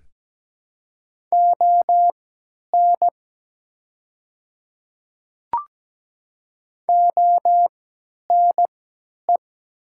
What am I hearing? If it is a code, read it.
Morse code, heard as a keyed tone, about 17 wpm, spaced as for 10 wpm, 700 Hz. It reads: ON ONE
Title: 3x Character Spacing - Top 200 Words - 17wpm